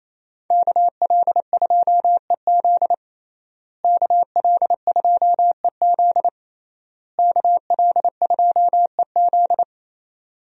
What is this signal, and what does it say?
Morse code, heard as a keyed tone, about 28 wpm, 700 Hz. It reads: KL2EZ KL2EZ KL2EZ
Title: USA Call Signs - 5 Characters - Repeated 3x - 28wpm